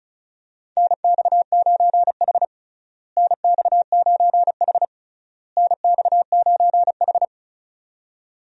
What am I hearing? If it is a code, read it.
Morse code, heard as a keyed tone, about 34 wpm, 700 Hz. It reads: NX9H NX9H NX9H
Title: USA Call Signs - 4 Characters - Repeated 3x - 35wpm